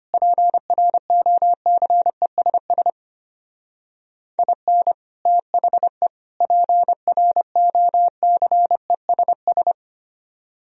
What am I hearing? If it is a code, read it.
Morse code, heard as a keyed tone, about 30 wpm, 700 Hz. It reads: PROCESS IN THE PROCESS
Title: Speed-Racing - Sets of 3 Words 25wpm